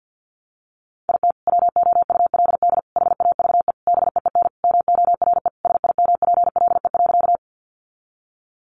Morse code, heard as a keyed tone, about 50 wpm, 700 Hz. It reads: IT WOULD HAVE BEEN MORE SIMPLE.
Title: Sentences from Top 400 Words 50wpm